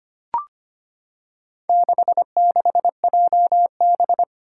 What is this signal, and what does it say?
Morse code, heard as a keyed tone, about 25 wpm, 700 Hz. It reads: 66JB
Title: Four Challenging Letter-Number Combinations 25wpm